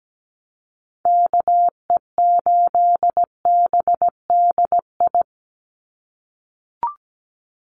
Morse code, heard as a keyed tone, about 17 wpm, 700 Hz. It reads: KE8BDI
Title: USA Call Signs 17wpm